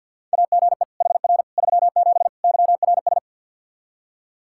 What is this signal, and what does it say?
Morse code, heard as a keyed tone, about 47 wpm, 700 Hz. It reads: AGE HR 37 YRS
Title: Practice QSO - Exchange 3 - More Info - 50wpm